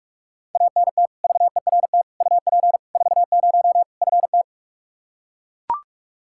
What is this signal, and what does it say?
Morse code, heard as a keyed tone, about 41 wpm, 700 Hz. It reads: ANT VERT UP 40 FT
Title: Practice QSO - Exchange 2 - Info - 45wpm